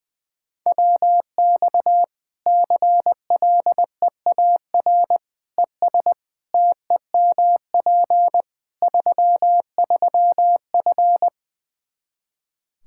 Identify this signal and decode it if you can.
Morse code, heard as a keyed tone, about 20 wpm, 700 Hz. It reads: WX CLEAR ES TEMP 33F
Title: Practice QSO - Exchange 2 - Info - 20wpm